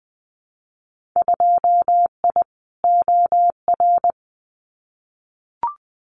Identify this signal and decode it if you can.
Morse code, heard as a keyed tone, about 20 wpm, 700 Hz. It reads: 2I OR
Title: Winter Field Day Exchanges Encore 20wpm